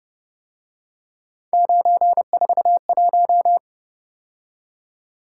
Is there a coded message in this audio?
Yes: Morse code from 1.5 to 3.6 s, heard as a keyed tone, about 29 wpm, 700 Hz. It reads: 941